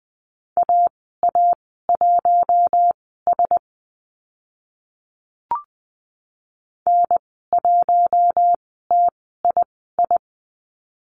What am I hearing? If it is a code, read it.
Morse code, heard as a keyed tone, about 20 wpm, spaced as for 14 wpm, 700 Hz. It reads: AA1S N1TII
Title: 042 Lesson 8g - Number 1 - Call Signs 20wpm